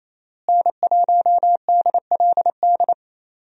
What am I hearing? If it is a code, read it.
Morse code, heard as a keyed tone, about 28 wpm, 700 Hz. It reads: N1DLD
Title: USA Call Signs - 5 Characters - Repeated 2x - 28wpm